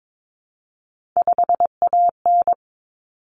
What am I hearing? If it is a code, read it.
Morse code, heard as a keyed tone, about 22 wpm, 700 Hz. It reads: 5AN
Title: Common Cut Numbers - 3-Digit 22wpm